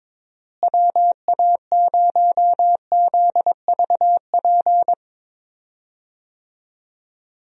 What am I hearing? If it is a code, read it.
Morse code, heard as a keyed tone, about 21 wpm, 700 Hz. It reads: WA0ZVP
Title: USA Call Signs - 6 Characters - Repeated 2x - 22wpm